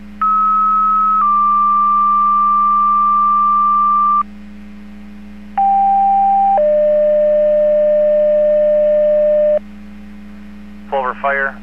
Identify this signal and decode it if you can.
Two-tone page, heard as a one-second tone, then a three-second tone, about 1290 then 1180 Hz, about 790 then 600 Hz, with speech. It is heard from 0.2 to 9.6 s.